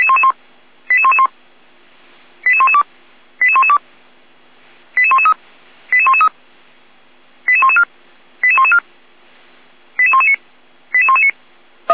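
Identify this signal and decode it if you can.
Five-tone selective call (ZVEI), heard as quick runs of five tones, tones of about 70 ms. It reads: 88111 88112 88112 88113 88113 88114 88114 88119 88119
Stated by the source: ZVEI five-tone selective call 88127 received via SDR.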